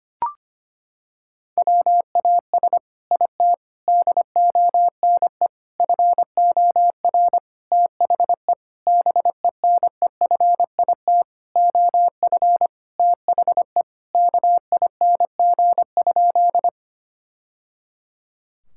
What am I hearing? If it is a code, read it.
Morse code, heard as a keyed tone, about 25 wpm, 700 Hz. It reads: WAS IT DONE FOR THE BENEFIT OF THE KING?
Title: Sentences from Top 1000 Words - Review Part 6 - 25wpm